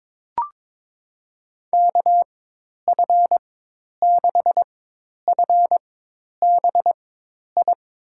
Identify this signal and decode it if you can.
Morse code, heard as a keyed tone, about 21 wpm, spaced as for 10 wpm, 700 Hz. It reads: KF6FBI
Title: USA Call Signs - 6 Characters - 4x Character Spacing - 22wpm